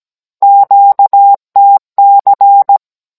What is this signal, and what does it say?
Morse code, heard as a keyed tone, about 17 wpm, 800 Hz. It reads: QTC
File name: Cours-NR19-16mm